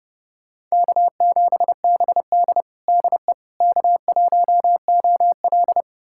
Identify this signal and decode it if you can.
Morse code, heard as a keyed tone, about 30 wpm, 700 Hz. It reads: K7BD DE K1OL
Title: USA Callsign DE Callsign - 4 Characters - 30wpm